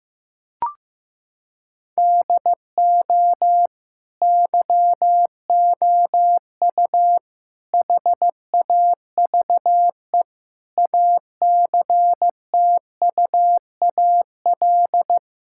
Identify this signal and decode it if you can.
Morse code, heard as a keyed tone, about 15 wpm, 700 Hz. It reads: DO YOU HAVE ACTUAL
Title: Sentences from Top 1000 Words - Review Part 5 - 15wpm